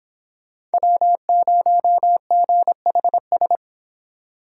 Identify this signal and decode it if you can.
Morse code, heard as a keyed tone, about 30 wpm, 700 Hz. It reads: W0GHS